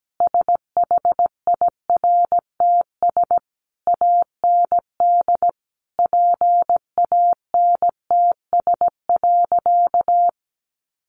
Morse code, heard as a keyed tone, about 17 wpm, 700 Hz. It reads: SHIRTS AND PANTS.
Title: A Sound Of Thunder 17wpm